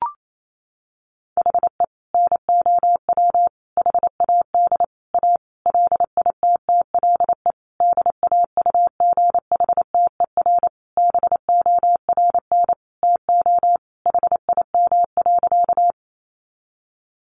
Morse code, heard as a keyed tone, about 28 wpm, 700 Hz. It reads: HE NOW HAD A LITTLE DAUGHTER BORN TO HIM.
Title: Sentences from Top 1500 Words 28wpm